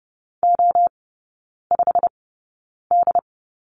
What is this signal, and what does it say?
Morse code, heard as a keyed tone, about 30 wpm, spaced as for 9 wpm, 690 Hz. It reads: O5D
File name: Lesson 17e - Letter u - Three Characters 30wpm